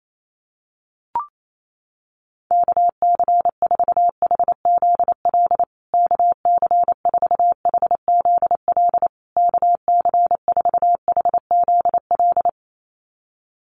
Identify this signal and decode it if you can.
Morse code, heard as a keyed tone, about 28 wpm, 700 Hz. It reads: KC4HZL KC4HZL KC4HZL
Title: USA Call Signs Repeated 3x 28wpm